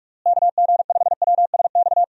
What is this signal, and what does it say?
Morse code, heard as a keyed tone, about 46 wpm, 700 Hz. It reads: KG5WSX
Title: USA Call Signs - 6 Characters - Repeated 3x - 45wpm